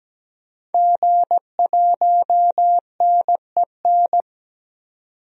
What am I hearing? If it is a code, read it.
Morse code, heard as a keyed tone, about 17 wpm, 700 Hz. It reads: G1NEN
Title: DXCC-100 Call Signs - 5 Characters - 17wpm